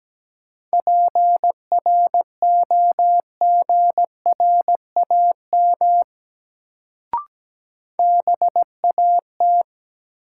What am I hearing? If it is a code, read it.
Morse code, heard as a keyed tone, about 17 wpm, 700 Hz. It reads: PROGRAM BAT
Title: Sentences from Top 1500 Words - New Word Drill Alt - 17wpm